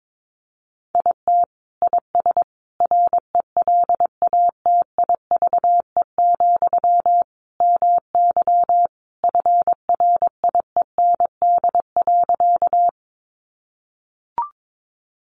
Morse code, heard as a keyed tone, about 22 wpm, 700 Hz. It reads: IT IS RELATIVE, MY FRIEND.
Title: Sentences from Top 1400 Words 22wpm